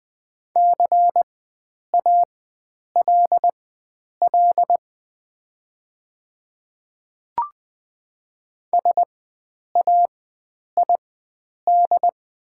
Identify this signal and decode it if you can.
Morse code, heard as a keyed tone, about 20 wpm, spaced as for 9 wpm, 690 Hz. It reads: CALL SAID
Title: 4x Character Spacing - Top 100 Words - 20wpm